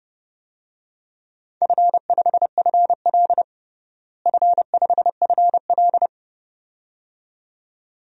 Morse code, heard as a keyed tone, about 32 wpm, 700 Hz. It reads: F5FL F5FL
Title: DXCC-100 Call Signs - 4 Characters - Repeated 2x - 30wpm